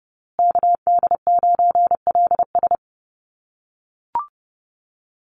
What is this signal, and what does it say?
Morse code, heard as a keyed tone, about 30 wpm, 700 Hz. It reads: KD9LS
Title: USA Call Signs Repeated 2x 30wpm